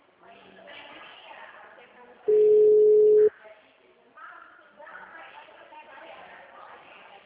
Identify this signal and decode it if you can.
Ringback tone (Europe, 425 Hz), with speech.
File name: out- -2010